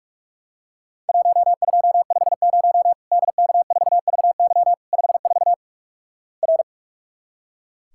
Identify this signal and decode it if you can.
Morse code, heard as a keyed tone, about 44 wpm, 700 Hz. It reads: 1250 DK4VY 54 R
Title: Worked All Europe DX Contest - QTC Exchange - 45wpm